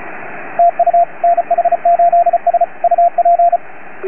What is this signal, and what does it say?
Morse code, heard as a keyed tone, about 34 wpm, 690 Hz, with radio noise. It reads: TU NH8S UP